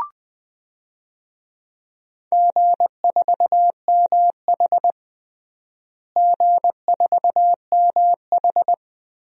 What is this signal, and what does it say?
Morse code, heard as a keyed tone, about 20 wpm, 700 Hz. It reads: G4MH G4MH
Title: DXCC-100 Call Signs - 4 Characters - Repeated 2x - 20wpm